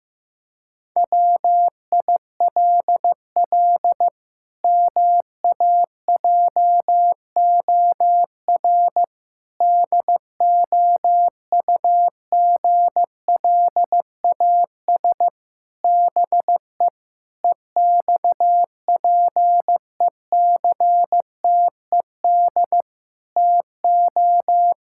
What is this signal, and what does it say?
Morse code, heard as a keyed tone, about 15 wpm, 700 Hz. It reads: WILL MAJOR DOUGLAS BE EXPECTED TO
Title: Pangram Challenge v2 - 15wpm